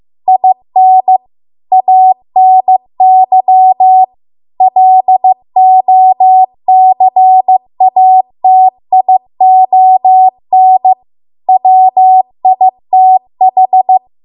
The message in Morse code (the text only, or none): IN ANY LOCATION WITH